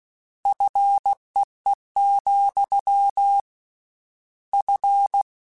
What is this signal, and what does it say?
Morse code, heard as a keyed tone, about 16 wpm, 800 Hz. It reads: FEE, F
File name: Combinations / Combinations-mp3s / Combinations@15